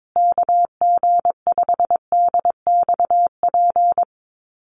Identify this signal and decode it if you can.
Morse code, heard as a keyed tone, about 22 wpm, 700 Hz. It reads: KG5DXP
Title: USA Call Signs Repeated 3x 22wpm